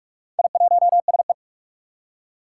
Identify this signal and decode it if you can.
Morse code, heard as a keyed tone, about 44 wpm, 700 Hz. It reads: I1SE
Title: Four Challenging Letter-Number Combinations 45wpm